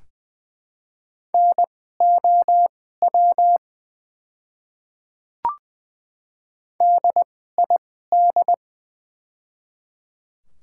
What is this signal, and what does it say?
Morse code, heard as a keyed tone, about 20 wpm, spaced as for 15 wpm, 690 Hz. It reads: NOW DID
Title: 2x Character Spacing - Top 100 Words - 20wpm